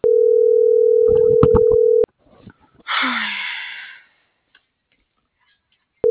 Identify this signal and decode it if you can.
Ringback tone (North America), 440 + 480 Hz.